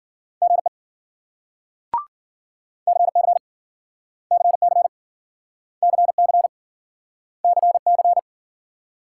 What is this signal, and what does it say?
Morse code, heard as a keyed tone, about 53 wpm, 700 Hz. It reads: GE CC CC CC CC